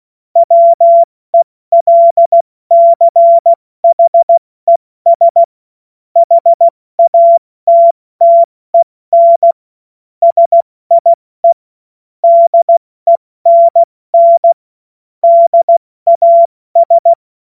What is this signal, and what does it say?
Morse code, heard as a keyed tone, about 16 wpm, 680 Hz. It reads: WELCHES HATTEN SIE DENN DAS
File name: Morsekurs / Witze / Witz 10 Juni Mai 2025 70 F80